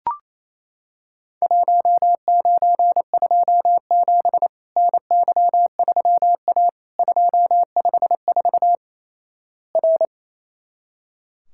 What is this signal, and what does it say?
Morse code, heard as a keyed tone, about 27 wpm, 700 Hz. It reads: 1927 NY3A 254 R
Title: Worked All Europe DX Contest - QTC Exchange - 28wpm